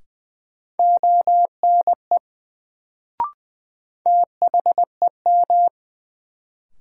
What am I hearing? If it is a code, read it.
Morse code, heard as a keyed tone, about 20 wpm, 700 Hz. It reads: ONE THEM
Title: Sets of 2 Words - New Word Drill - 20wpm